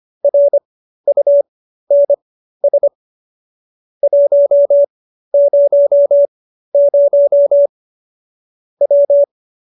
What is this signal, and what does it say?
Morse code, heard as a keyed tone, about 25 wpm, spaced as for 13 wpm, 570 Hz. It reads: RUNS 100 W